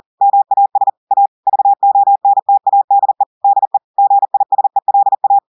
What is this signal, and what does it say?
Morse code, heard as a keyed tone, about 40 wpm, 800 Hz. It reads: MAS A VONTADE DE GISELA
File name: audios / portugais / portugais texte en clair 1 vitesse 40 mots minutes